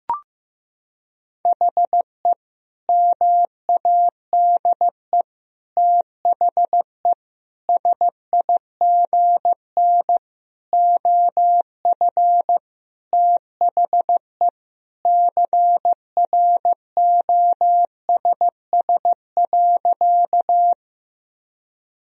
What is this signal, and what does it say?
Morse code, heard as a keyed tone, about 16 wpm, 700 Hz. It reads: HE MADE THE SIGN OF THE CROSS.
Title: Sentences from Top 1000 Words - Review Part 1 - 15wpm